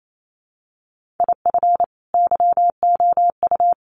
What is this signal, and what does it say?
Morse code, heard as a keyed tone, about 28 wpm, 700 Hz. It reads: IF YOU
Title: Sentences from Top 1500 Words 28wpm